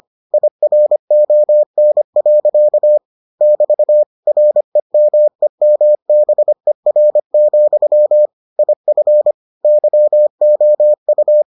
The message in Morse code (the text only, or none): IRON. <BT> REMEMBER, IF YOU